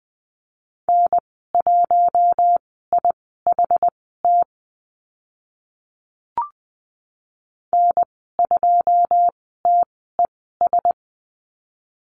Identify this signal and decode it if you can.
Morse code, heard as a keyed tone, about 20 wpm, spaced as for 14 wpm, 700 Hz. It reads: N1IHT N2TES
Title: 082 Lesson 14g - Number 2 - Call Signs 20wpm